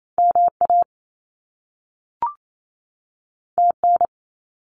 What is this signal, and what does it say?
Morse code, heard as a keyed tone, about 28 wpm, 700 Hz. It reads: MA TN
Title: US States 28wpm